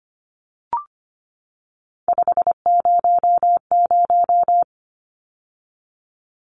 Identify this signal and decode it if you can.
Morse code, heard as a keyed tone, about 25 wpm, 700 Hz. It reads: 500
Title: Numbers - 3-Digit 25wpm